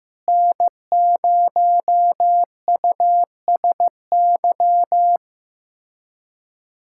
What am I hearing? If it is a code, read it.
Morse code, heard as a keyed tone, about 15 wpm, 700 Hz. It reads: N0USY
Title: USA Callsign DE Callsign - 5 Characters - 15wpm